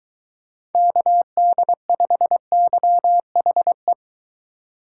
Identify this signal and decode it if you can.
Morse code, heard as a keyed tone, about 23 wpm, 700 Hz. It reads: KD5YHE